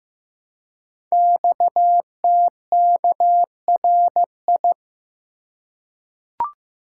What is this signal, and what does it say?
Morse code, heard as a keyed tone, about 15 wpm, 700 Hz. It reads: XTKRI
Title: Code Groups - 5 Letters - 15wpm